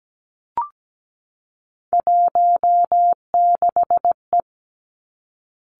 Morse code, heard as a keyed tone, about 17 wpm, 700 Hz. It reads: 16E
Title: Three Challenging Letter-Number Combinations 17wpm